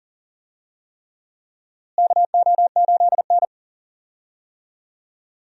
Morse code, heard as a keyed tone, about 40 wpm, 700 Hz. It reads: KO8N
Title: USA Call Signs - 4 Characters - Repeated 3x - 40wpm